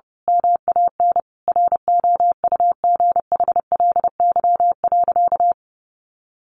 Morse code, heard as a keyed tone, about 30 wpm, 700 Hz. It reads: MAN ROUGHLY.
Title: Sentences from Top 1500 Words 30wpm